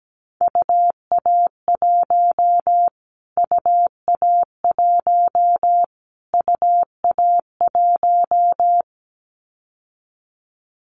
Morse code, heard as a keyed tone, about 17 wpm, 700 Hz. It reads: UA1 UA1 UA1
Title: Callsign Prefixes for Top 25 DXCC - Repeated 3x - 17wpm